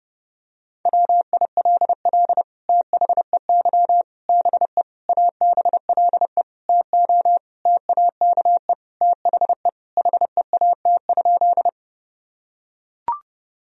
Morse code, heard as a keyed tone, about 30 wpm, 700 Hz. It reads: WILL THEY BE ABLE TO TAKE THE HEAT?
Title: Sentences from Top 1000 Words - Review Part 3 - 30wpm